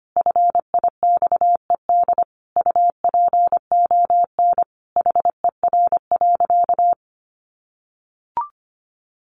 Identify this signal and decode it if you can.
Morse code, heard as a keyed tone, about 25 wpm, 700 Hz. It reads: FIXED UPON HER.
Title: Sentences from Top 1100 Words 25wpm